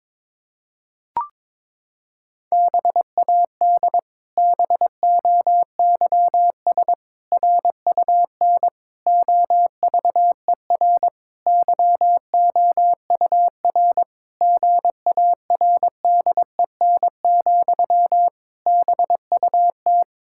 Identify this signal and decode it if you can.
Morse code, heard as a keyed tone, about 22 wpm, 700 Hz. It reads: BAD BOYS RUN OVER YOUR GARDEN, BUT
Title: Pangram Challenge v2 - 22wpm